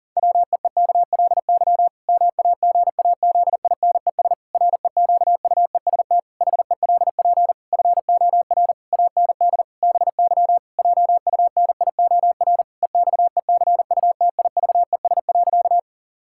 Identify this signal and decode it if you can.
Morse code, heard as a keyed tone, about 38 wpm, 700 Hz. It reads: WEEKLY MAGAZINES REQUEST HELP FOR AND BY JUNIOR EXECUTIVES.
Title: Pangram Challenge v2 - 40wpm